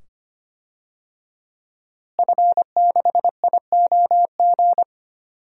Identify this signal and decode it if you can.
Morse code, heard as a keyed tone, about 25 wpm, 700 Hz. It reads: F6IOG
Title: DXCC-100 Call Signs - 5 Characters - Repeated 2x - 25wpm